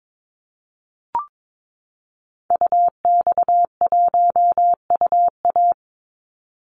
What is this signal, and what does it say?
Morse code, heard as a keyed tone, about 22 wpm, 700 Hz. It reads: UX1UA
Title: Call Signs Repeated 3x - 22wpm